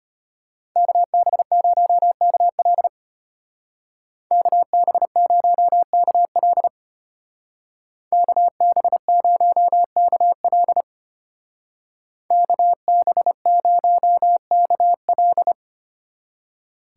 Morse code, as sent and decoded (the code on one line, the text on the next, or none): -.- -... ----- -.- .-.. / -.- -... ----- -.- .-.. / -.- -... ----- -.- .-.. / -.- -... ----- -.- .-..
KB0KL KB0KL KB0KL KB0KL